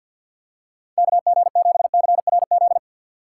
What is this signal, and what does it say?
Morse code, heard as a keyed tone, about 52 wpm, 700 Hz. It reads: KG7CRZ
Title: USA Call Signs - 6x Character Spacing - 50wpm